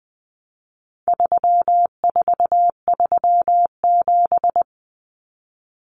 Morse code, heard as a keyed tone, about 20 wpm, 700 Hz. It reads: 3437
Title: Common Cut Numbers - 4-Digit 20wpm